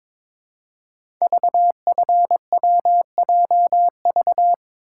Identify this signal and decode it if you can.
Morse code, heard as a keyed tone, about 22 wpm, 690 Hz. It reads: VFWJV
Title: Code Groups - 5 Letters - 22wpm